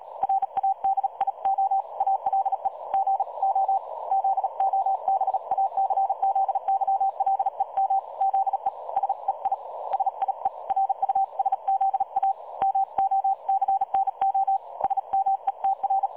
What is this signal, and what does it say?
Morse code, heard as a keyed tone, about 38 wpm, 780 Hz. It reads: RADIO ALE GO ZOBACZYLEM ZE SEI SIE RUSZA MOCNO SMETR
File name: dlazaawansowanych / nagraniazpasma / websdr recording 3565.0kHz